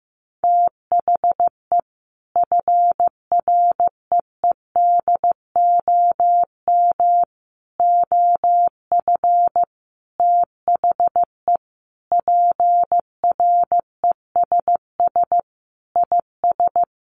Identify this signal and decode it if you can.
Morse code, heard as a keyed tone, about 15 wpm, 700 Hz. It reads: THE FREEDOM OF THE PRESS IS
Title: Sentences from Top 800 Words 15wpm